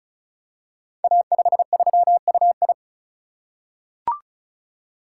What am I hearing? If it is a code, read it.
Morse code, heard as a keyed tone, about 36 wpm, 690 Hz. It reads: A53UI
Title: Code Groups - 5 Letter:Numbers - 35wpm